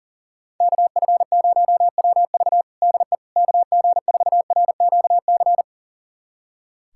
Morse code, heard as a keyed tone, about 37 wpm, 700 Hz. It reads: KF0WV DE KG4RQC
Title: USA Callsign DE Callsign 40wpm